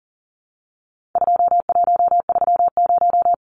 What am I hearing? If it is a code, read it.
Morse code, heard as a keyed tone, about 40 wpm, 700 Hz. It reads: 2130
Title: Numbers - 4-Digit 40wpm